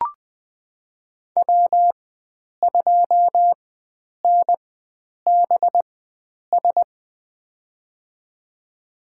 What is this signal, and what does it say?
Morse code, heard as a keyed tone, about 20 wpm, spaced as for 9 wpm, 700 Hz. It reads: W2NBS
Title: USA Call Signs - 4x Character Spacing - 20wpm